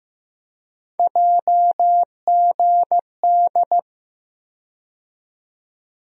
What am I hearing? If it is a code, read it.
Morse code, heard as a keyed tone, about 15 wpm, 700 Hz. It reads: JGD
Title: USA Call Sign Postfixes - 3 Letters 15wpm